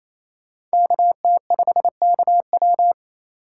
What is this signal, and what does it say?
Morse code, heard as a keyed tone, about 36 wpm, 700 Hz. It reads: KT5KW